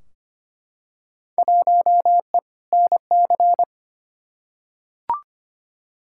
Morse code, heard as a keyed tone, about 25 wpm, 700 Hz. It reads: 1E NC
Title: Field Day Exchanges - 25wpm